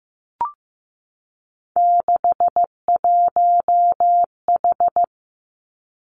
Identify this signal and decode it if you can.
Morse code, heard as a keyed tone, about 15 wpm, 700 Hz. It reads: 61H
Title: Three Challenging Letter-Number Combinations 15wpm